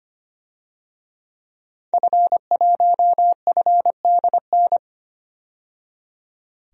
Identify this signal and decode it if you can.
Morse code, heard as a keyed tone, about 25 wpm, 700 Hz. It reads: F1FDN
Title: DXCC-100 Call Signs - 5 Characters - 25wpm